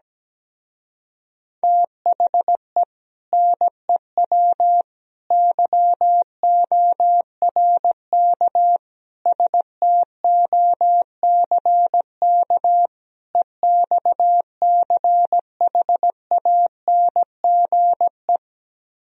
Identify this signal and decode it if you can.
Morse code, heard as a keyed tone, about 17 wpm, 700 Hz. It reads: THE NEW YORK STOCK EXCHANGE
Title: Speed-Racing - Sets of 5 Words 17wpm